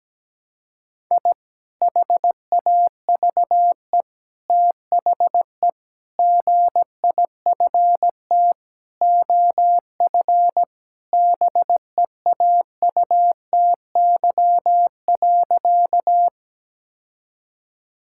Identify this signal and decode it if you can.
Morse code, heard as a keyed tone, about 17 wpm, 700 Hz. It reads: I HAVE THE GIFT OF BEAUTY.
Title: Sentences from Top 1000 Words - Review Part 6 - 17wpm